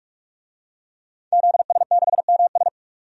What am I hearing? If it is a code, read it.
Morse code, heard as a keyed tone, about 44 wpm, 700 Hz. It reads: ZS6MS
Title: DXCC-100 Call Signs - 5 Characters - 45wpm